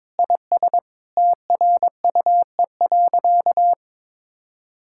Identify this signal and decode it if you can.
Morse code, heard as a keyed tone, about 21 wpm, 700 Hz. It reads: IS TRUE.
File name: practice / 22wpm / Sentences with 4-Word Phrases - 22wpm